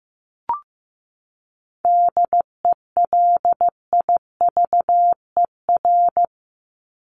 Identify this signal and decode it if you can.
Morse code, heard as a keyed tone, about 15 wpm, 700 Hz. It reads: DELIVER
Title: Top 1751-2000 Words 15wpm